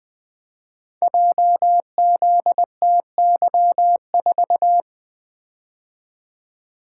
Morse code, heard as a keyed tone, about 20 wpm, 700 Hz. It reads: JZTY4
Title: Code Groups - 5 Letter:Numbers - 20wpm